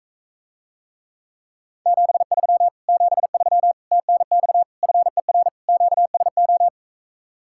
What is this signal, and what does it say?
Morse code, heard as a keyed tone, about 45 wpm, 700 Hz. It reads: 73 73 TNX FER QSO